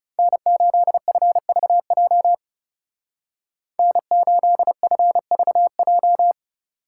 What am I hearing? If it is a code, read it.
Morse code, heard as a keyed tone, about 37 wpm, 700 Hz. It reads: N8FVJ N8FVJ
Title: Speed Racing - USA Call Signs 30wpm